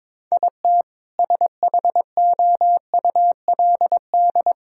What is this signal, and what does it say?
Morse code, heard as a keyed tone, about 21 wpm, 700 Hz. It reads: IT SHOULD
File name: practice / 22wpm / Sets of 6 Words - 22wpm